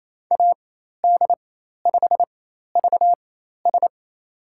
Morse code, heard as a keyed tone, about 28 wpm, spaced as for 13 wpm, 700 Hz. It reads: AD5VS
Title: USA Call Signs - 4x Character Spacing - 28wpm